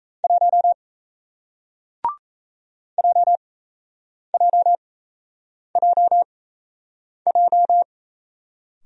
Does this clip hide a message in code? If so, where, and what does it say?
Yes: Morse code from 0.2 to 7.8 s, heard as a keyed tone, about 38 wpm, spaced as for 8 wpm, 700 Hz. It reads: 1 JJJJ